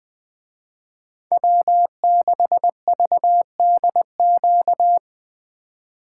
Morse code, heard as a keyed tone, about 20 wpm, 700 Hz. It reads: W6VDQ
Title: Speed Racing - USA 1x3 Call Signs 17wpm